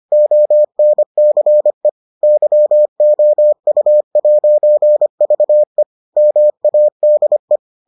Morse code, heard as a keyed tone, about 25 wpm, 600 Hz. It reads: ONCE YOU'VE MADE